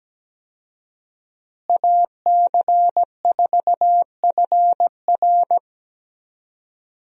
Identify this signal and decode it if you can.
Morse code, heard as a keyed tone, about 17 wpm, 700 Hz. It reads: AC4FR
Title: USA Call Signs - 5 Characters - Repeated 2x - 17wpm